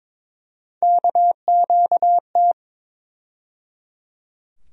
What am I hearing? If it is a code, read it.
Morse code, heard as a keyed tone, about 22 wpm, 700 Hz. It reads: KQT
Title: USA Call Sign Postfixes - 3 Letters 22wpm